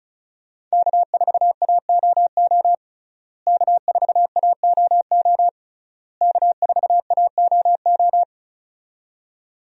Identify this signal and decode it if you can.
Morse code, heard as a keyed tone, about 34 wpm, 700 Hz. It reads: K4AOO K4AOO K4AOO
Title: USA Call Signs - 5 Characters - Repeated 3x - 35wpm